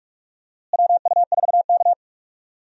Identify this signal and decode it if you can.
Morse code, heard as a keyed tone, about 46 wpm, 700 Hz. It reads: WU4K
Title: USA Call Signs - 4 Characters - Repeated 3x - 45wpm